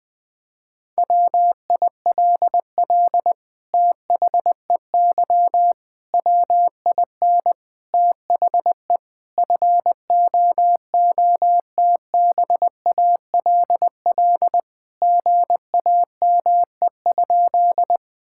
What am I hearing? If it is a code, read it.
Morse code, heard as a keyed tone, about 21 wpm, 700 Hz. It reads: WILL THEY WIN THE FOOTBALL GAME?
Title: Sentences from Top 1600 Words 20wpm